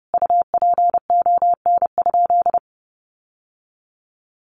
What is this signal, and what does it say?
Morse code, heard as a keyed tone, about 30 wpm, 700 Hz. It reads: UPON?
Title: Sentences from Top 1100 Words 30wpm